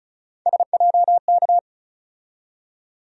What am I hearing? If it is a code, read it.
Morse code, heard as a keyed tone, about 34 wpm, 700 Hz. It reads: SJK
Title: USA Call Sign Postfixes - 3 Letters 35wpm